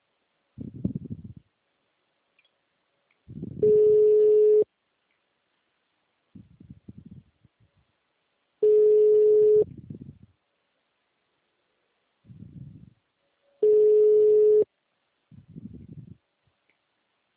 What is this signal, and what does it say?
Ringback tone (Europe, 425 Hz).